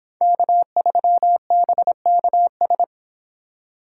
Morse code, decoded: K3BKS

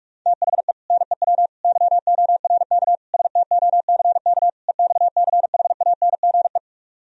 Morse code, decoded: THE NEW YORK STOCK EXCHANGE

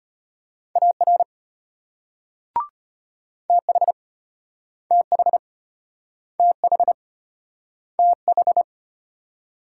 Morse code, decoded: AR TH TH TH TH

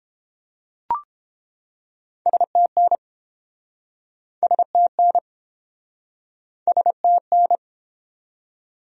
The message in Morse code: STN STN STN

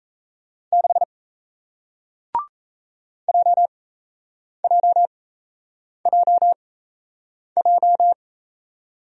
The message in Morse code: 6 JJJJ

